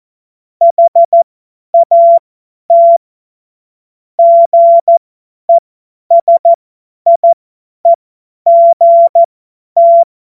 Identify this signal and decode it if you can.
Morse code, heard as a keyed tone, about 14 wpm, spaced as for 10 wpm, 680 Hz. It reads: HAT GESIEGT
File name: Morsekurs / Witze / Witz 09. August 2022 50 F70